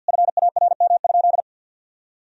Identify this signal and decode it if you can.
Morse code, heard as a keyed tone, about 50 wpm, 700 Hz. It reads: FARM?
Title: Sentences from Top 1000 Words - Review Part 2 - 50wpm